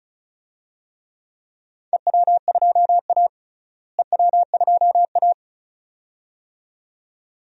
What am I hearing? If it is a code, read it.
Morse code, heard as a keyed tone, about 36 wpm, 700 Hz. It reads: EW2A EW2A
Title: DXCC-100 Call Signs - 4 Characters - Repeated 2x - 35wpm